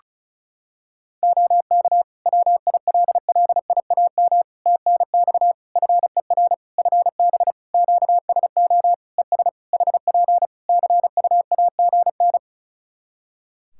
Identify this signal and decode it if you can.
Morse code, heard as a keyed tone, about 36 wpm, 700 Hz. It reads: OK WILLIAM TNX FER FB QSO ES HP CUAGN
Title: Practice QSO - Exchange 4 - Ending - 35wpm